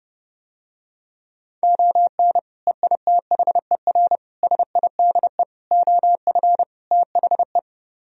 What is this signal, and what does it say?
Morse code, heard as a keyed tone, about 32 wpm, 700 Hz. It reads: ON EITHER SIDE OF THE